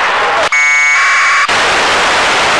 Packet radio (AX.25, as in APRS).
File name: lu4aao / globo19 1 LU2DPW